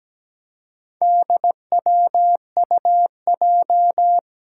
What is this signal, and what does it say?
Morse code, heard as a keyed tone, about 17 wpm, 700 Hz. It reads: DWUJ
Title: Four Challenging Letter-Number Combinations 17wpm